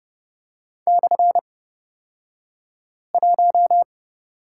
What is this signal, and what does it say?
Morse code, heard as a keyed tone, about 31 wpm, spaced as for 5 wpm, 700 Hz. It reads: /1
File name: Lesson 37d - Letter x - Two Characters 30wpm